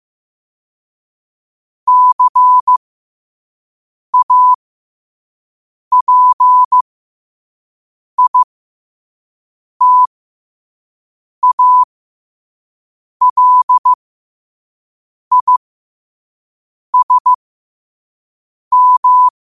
Morse code, heard as a keyed tone, about 15 wpm, spaced as for 5 wpm, 1000 Hz. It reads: CAPITALISM
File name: qotdfiles / 5wpm